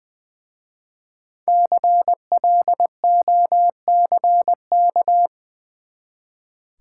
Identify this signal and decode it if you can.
Morse code, heard as a keyed tone, about 20 wpm, 700 Hz. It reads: CLOCK
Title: Easy Words with Hints - 20wpm